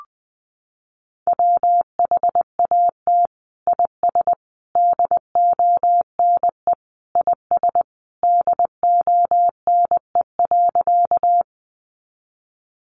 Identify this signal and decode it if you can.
Morse code, heard as a keyed tone, about 20 wpm, 700 Hz. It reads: WHAT IS DONE IS DONE.
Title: Sentences from Top 500 Words 20wpm